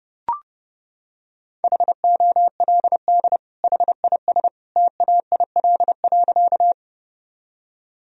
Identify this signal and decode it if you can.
Morse code, heard as a keyed tone, about 30 wpm, 700 Hz. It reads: HOLD HIS TAIL.
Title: Sentences from Top 1000 Words - Review Part 3 - 30wpm